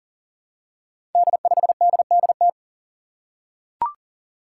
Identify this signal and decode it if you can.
Morse code, heard as a keyed tone, about 38 wpm, 690 Hz. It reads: D5DDT
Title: Five Challenging Letter-Number Combinations 40wpm